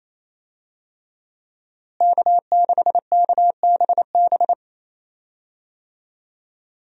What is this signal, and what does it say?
Morse code, heard as a keyed tone, about 28 wpm, 700 Hz. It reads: K6KBB
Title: USA Call Signs - 5 Characters - Repeated 3x - 28wpm